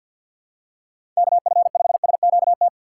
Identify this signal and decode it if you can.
Morse code, heard as a keyed tone, about 51 wpm, 700 Hz. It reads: KF5SQT
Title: USA Call Signs - 6 Characters - 6x Character Spacing - 50wpm